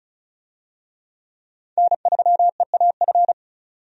Morse code, heard as a keyed tone, about 36 wpm, 700 Hz. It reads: N3EAF
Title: USA Call Signs - 5 Characters - 35wpm